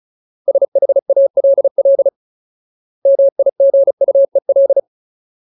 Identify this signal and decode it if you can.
Morse code, heard as a keyed tone, about 34 wpm, 550 Hz. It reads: SHALL MIGUEL
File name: Ultimate Word List / 35WPM Ultimate